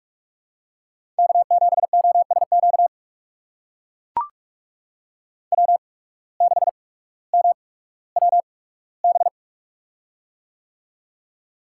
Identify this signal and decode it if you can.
Morse code, heard as a keyed tone, about 42 wpm, spaced as for 19 wpm, 700 Hz. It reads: K7OSQ W6MWB
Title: USA Call Signs - 5 Characters - 8x Character Spacing - 45wpm